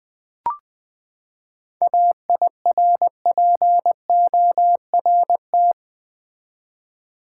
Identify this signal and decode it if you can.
Morse code, heard as a keyed tone, about 20 wpm, 700 Hz. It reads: AIRPORT
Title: Seven Letter Words - 20wpm